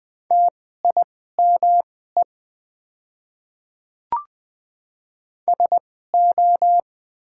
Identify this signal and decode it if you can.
Morse code, heard as a keyed tone, about 20 wpm, spaced as for 14 wpm, 690 Hz. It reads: TIME SO
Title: 2x Character Spacing - Top 100 Words - 20wpm